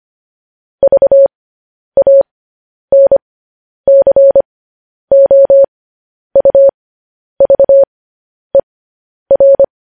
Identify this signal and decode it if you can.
Morse code, heard as a keyed tone, about 25 wpm, spaced as for 10 wpm, 550 Hz. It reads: VANCOUVER
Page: news headlines in morse code at 10 wpm